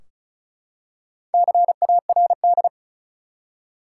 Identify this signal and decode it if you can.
Morse code, heard as a keyed tone, about 34 wpm, 700 Hz. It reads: CARD
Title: Four Letter Words - 35wpm